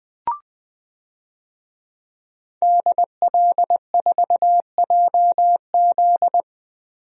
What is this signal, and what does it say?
Morse code, heard as a keyed tone, about 21 wpm, 700 Hz. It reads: DL4JZ
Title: DXCC-100 Call Signs - 5 Characters - Repeated 3x - 20wpm